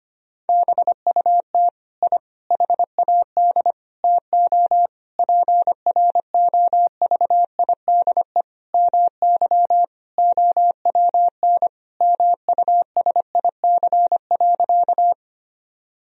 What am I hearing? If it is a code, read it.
Morse code, heard as a keyed tone, about 25 wpm, 700 Hz. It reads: BUT I HAD TO PROVIDE MY OWN MUSIC.
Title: Sentences from Top 1000 Words - Review Part 3 - 25wpm